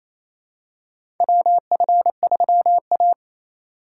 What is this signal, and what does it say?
Morse code, heard as a keyed tone, about 28 wpm, 700 Hz. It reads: WF3A
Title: USA Call Signs - 4x Character Spacing - 28wpm